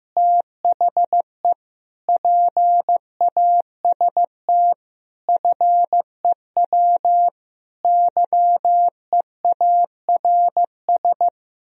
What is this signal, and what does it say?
Morse code, heard as a keyed tone, about 15 wpm, 700 Hz. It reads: THE PAST FEW YEARS
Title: Speed-Racing - Sets of 5 Words 15wpm